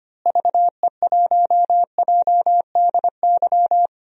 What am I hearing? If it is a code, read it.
Morse code, heard as a keyed tone, about 25 wpm, 700 Hz. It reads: VE1JDY
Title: DXCC-100 Call Signs - 6 Characters - Repeated 2x - 25wpm